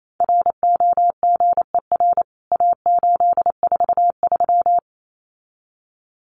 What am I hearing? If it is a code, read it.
Morse code, heard as a keyed tone, about 28 wpm, 700 Hz. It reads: ROGER A843
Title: CWT Exchange 28wpm